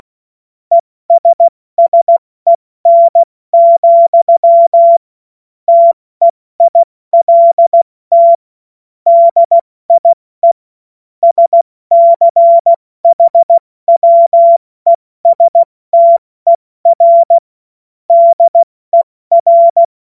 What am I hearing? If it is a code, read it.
Morse code, heard as a keyed tone, about 16 wpm, 680 Hz. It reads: ESSEN, TEILT DIE SCHWESTER DER
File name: Morsekurs / Witze / Witz 27. Juni 2023 70 F80